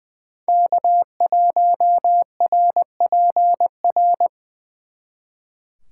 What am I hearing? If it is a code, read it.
Morse code, heard as a keyed tone, about 24 wpm, 700 Hz. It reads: K1RPR
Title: Speed Racing - USA Call Signs 20wpm